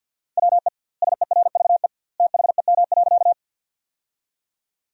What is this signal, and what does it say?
Morse code, heard as a keyed tone, about 50 wpm, 700 Hz. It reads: WE SERVE THEM.
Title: 2x Word-Spacing - Sentences from Top 400 Words - 50wpm